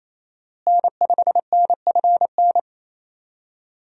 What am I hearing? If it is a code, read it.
Morse code, heard as a keyed tone, about 27 wpm, 700 Hz. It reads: N5NFN
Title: USA Call Signs - 5 Characters - Repeated 3x - 28wpm